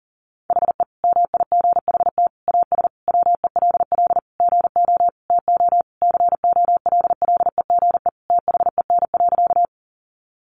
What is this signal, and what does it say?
Morse code, heard as a keyed tone, about 37 wpm, 700 Hz. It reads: HE MIGHT AS WELL GO TO COLLEGE THEN.
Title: Sentences from Top 1300 Words 40wpm